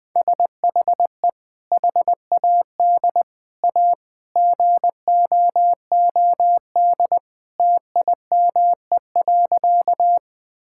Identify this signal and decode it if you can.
Morse code, heard as a keyed tone, about 20 wpm, 700 Hz. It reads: SHE HAD A GOOD TIME.
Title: Sentences with 2-Word Phrases - 20wpm